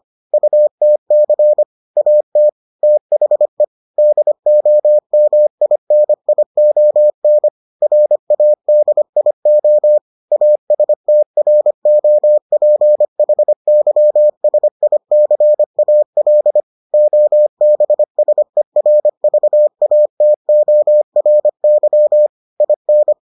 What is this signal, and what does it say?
Morse code, heard as a keyed tone, about 25 wpm, 600 Hz. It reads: UTC AT THE DOMINION RADIO ASTROPHYSICAL OBSERVATORY IN